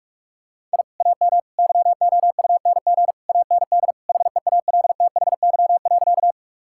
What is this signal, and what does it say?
Morse code, heard as a keyed tone, about 46 wpm, 700 Hz. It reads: I AM YOUNG AND HEALTHY.
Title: Sentences from Top 1000 Words - Review Part 5 - 45wpm